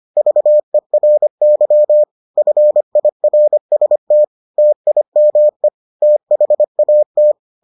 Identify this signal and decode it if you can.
Morse code, heard as a keyed tone, about 25 wpm, 600 Hz. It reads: VERY FIRST TIME THAT